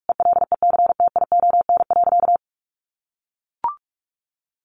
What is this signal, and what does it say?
Morse code, heard as a keyed tone, about 44 wpm, 700 Hz. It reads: ELECTION.
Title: Sentences from Top 1500 Words 45wpm